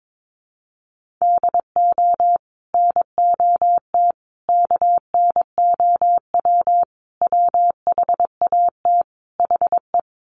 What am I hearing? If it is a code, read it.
Morse code, heard as a keyed tone, about 22 wpm, 700 Hz. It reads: DO NOT KNOW WHAT HE
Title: Sets of 5 Words Encore2 22wpm